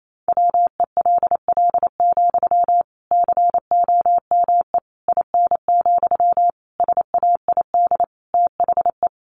Morse code, heard as a keyed tone, about 28 wpm, 700 Hz. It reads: WELL, COME IN, SAID THE
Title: Sentences from Top 1000 Words 28wpm